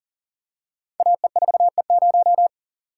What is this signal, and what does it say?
Morse code, heard as a keyed tone, about 42 wpm, 700 Hz. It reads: AE4E0
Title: Code Groups - 5 Letter:Numbers - 40wpm